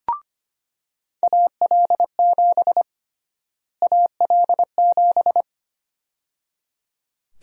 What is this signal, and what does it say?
Morse code, heard as a keyed tone, about 25 wpm, 700 Hz. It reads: AL7 AL7
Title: USA Call Sign Prefixes - Repeated 2x - 25wpm